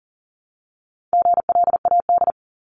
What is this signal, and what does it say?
Morse code, heard as a keyed tone, about 40 wpm, 700 Hz. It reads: GLAD
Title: Top 751-1000 Words 40wpm